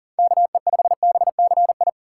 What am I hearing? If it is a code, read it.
Morse code, heard as a keyed tone, about 40 wpm, 700 Hz. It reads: KE5BCI
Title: USA Call Signs - 6 Characters - 4x Character Spacing - 40wpm